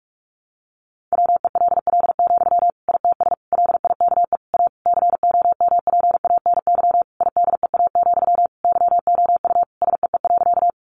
Morse code, heard as a keyed tone, about 44 wpm, 700 Hz. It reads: WELL, ITS LIKE A COMPANY IDEA, YOU SEE.
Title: Sentences from Top 600 Words 45wpm